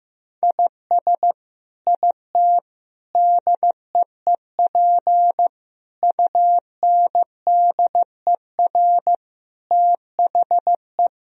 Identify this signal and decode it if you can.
Morse code, heard as a keyed tone, about 15 wpm, 700 Hz. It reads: IS IT DEEP UNDER THE
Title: Sentences from Top 1000 Words - Review Part 3 - 15wpm